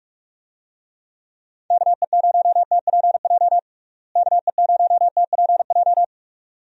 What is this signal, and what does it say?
Morse code, heard as a keyed tone, about 46 wpm, 700 Hz. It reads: KE0TPJ KE0TPJ
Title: USA Call Signs - 6 Characters - Repeated 2x - 45wpm